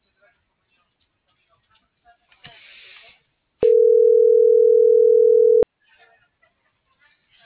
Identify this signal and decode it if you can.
Ringback tone (North America), 440 + 480 Hz.